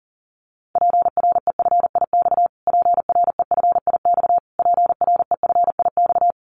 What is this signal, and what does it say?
Morse code, heard as a keyed tone, about 37 wpm, 700 Hz. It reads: PREFIX PREFIX PREFIX
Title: Callsign Prefixes for Top 25 DXCC - Repeated 3x - 40wpm